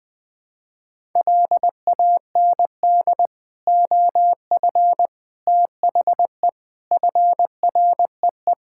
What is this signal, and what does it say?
Morse code, heard as a keyed tone, about 20 wpm, 700 Hz. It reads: LAND OF THE FREE
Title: Sets of 9 Words - 20wpm